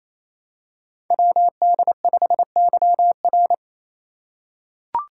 Morse code, heard as a keyed tone, about 27 wpm, 700 Hz. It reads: WD5YR
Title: Code Groups - 5 Letter:Numbers - 28wpm